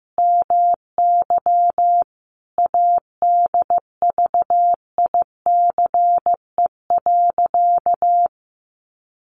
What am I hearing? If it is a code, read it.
Morse code, heard as a keyed tone, about 15 wpm, 700 Hz. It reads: MY ADVICE.
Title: Sentences from Top 1100 Words 15wpm